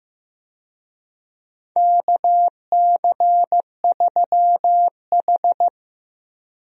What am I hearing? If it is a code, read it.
Morse code, heard as a keyed tone, about 16 wpm, 700 Hz. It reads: KC3H
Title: USA Call Signs - 4 Characters - Repeated 2x - 15wpm